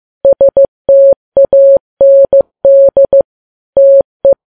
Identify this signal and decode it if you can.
Morse code, heard as a keyed tone, about 15 wpm, 550 Hz. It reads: STAND TE
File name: Files Web tls / Morse on9fv / morse15kps